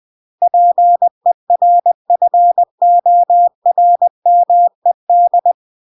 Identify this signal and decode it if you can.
Morse code, heard as a keyed tone, about 20 wpm, 700 Hz. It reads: PERFORMED
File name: CW / jodorowsky / 82 commandments 20-20